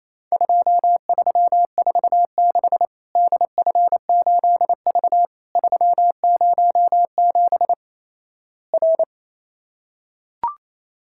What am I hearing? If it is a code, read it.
Morse code, heard as a keyed tone, about 28 wpm, 700 Hz. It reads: 2346 DF8V 307 R
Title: Worked All Europe DX Contest - QTC Exchange - 28wpm